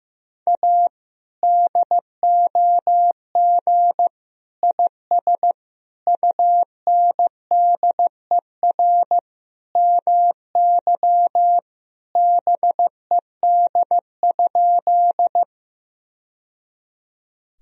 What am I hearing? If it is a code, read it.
Morse code, heard as a keyed tone, about 15 wpm, 700 Hz. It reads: A DOG IS UNDER MY BED?
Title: Sentences from Top 1000 Words - Review Part 2 - 15wpm